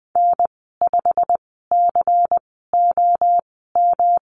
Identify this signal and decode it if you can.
Morse code, heard as a keyed tone, about 20 wpm, spaced as for 14 wpm, 700 Hz. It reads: N5COM
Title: 161 Lesson 26g - Letter g - Call Signs 20wpm